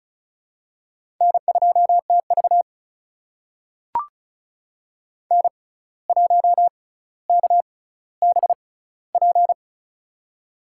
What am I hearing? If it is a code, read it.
Morse code, heard as a keyed tone, about 34 wpm, spaced as for 17 wpm, 700 Hz. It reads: N2TV N1KBP
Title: USA Call Signs - 6x Character Spacing - 35wpm